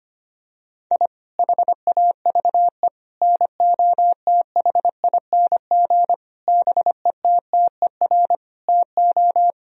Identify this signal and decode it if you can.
Morse code, heard as a keyed tone, about 25 wpm, 690 Hz. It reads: I HAVE NOTHING BETTER TO
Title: Sentences from Top 1000 Words - Review Part 4 - 25wpm